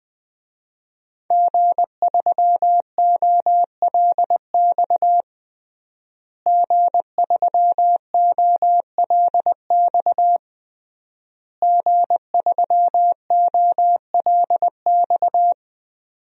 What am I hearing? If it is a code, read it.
Morse code, heard as a keyed tone, about 20 wpm, 700 Hz. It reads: G3OLX G3OLX G3OLX
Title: DXCC-100 Call Signs - 5 Characters - Repeated 3x - 20wpm